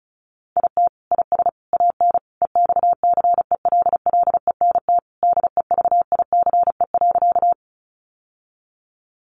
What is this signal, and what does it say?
Morse code, heard as a keyed tone, about 34 wpm, 700 Hz. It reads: IT IS AN EXCELLENT DEVICE.
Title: Sentences from Top 1500 Words 35wpm